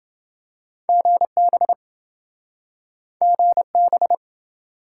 Morse code, heard as a keyed tone, about 26 wpm, 700 Hz. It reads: GB GB